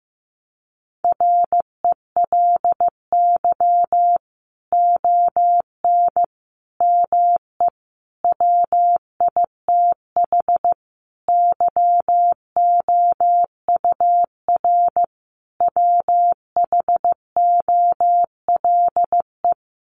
Morse code, heard as a keyed tone, about 15 wpm, 700 Hz. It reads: RELY ON ME WITH YOUR WHOLE
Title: Sentences from Top 1200 Words 15wpm